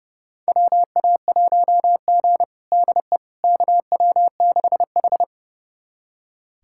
Morse code, heard as a keyed tone, about 32 wpm, 700 Hz. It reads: WA1G DE KW6H
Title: USA Callsign DE Callsign - 4 Characters - 30wpm